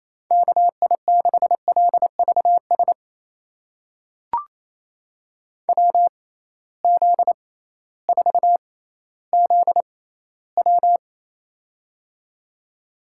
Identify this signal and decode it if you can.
Morse code, heard as a keyed tone, about 28 wpm, spaced as for 15 wpm, 700 Hz. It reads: KI6LVS WZ4ZW
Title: USA Call Signs - 6x Character Spacing - 28wpm